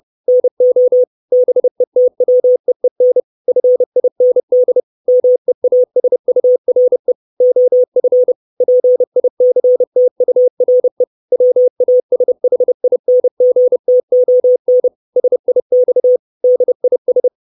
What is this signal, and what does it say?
Morse code, heard as a keyed tone, about 30 wpm, 500 Hz. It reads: NO BETWEEN FIND MEASURE OF PICTURE WASHINGTON SIX DIS
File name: 632 Words 30WPM